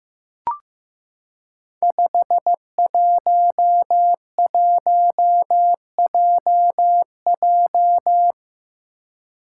Morse code, heard as a keyed tone, about 15 wpm, 700 Hz. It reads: 511JJ
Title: Five Challenging Letter-Number Combinations 15wpm